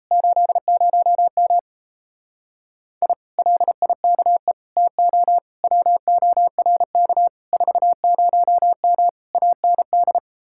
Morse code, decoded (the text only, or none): 80M I LIKE TO WORK 40M AND